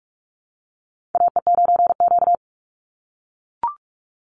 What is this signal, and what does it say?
Morse code, heard as a keyed tone, about 44 wpm, 700 Hz. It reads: AE9Q
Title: NA SOTA Chaser Callsigns 45wpm